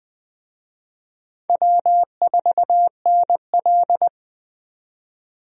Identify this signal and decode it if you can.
Morse code, heard as a keyed tone, about 20 wpm, 700 Hz. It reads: W4NL
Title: USA Call Signs - 4 Characters - 20wpm